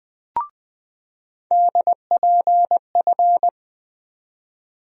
Morse code, heard as a keyed tone, about 20 wpm, 700 Hz. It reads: DPF
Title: USA Call Sign Postfixes - 3 Letters 20wpm